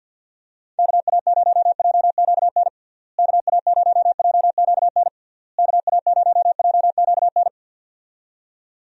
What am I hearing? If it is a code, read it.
Morse code, heard as a keyed tone, about 50 wpm, 700 Hz. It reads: KA0JQN KA0JQN KA0JQN
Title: USA Call Signs - 6 Characters - Repeated 3x - 50wpm